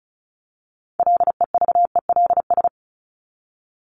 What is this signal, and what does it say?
Morse code, heard as a keyed tone, about 34 wpm, 700 Hz. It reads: LEVELS
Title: ES:S Suffix - Top 100 Words with Base Words 35wpm